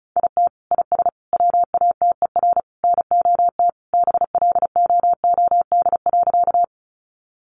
Morse code, heard as a keyed tone, about 34 wpm, 700 Hz. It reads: IT IS WATER NOT BLOOD.
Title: Sentences from Top 1100 Words 35wpm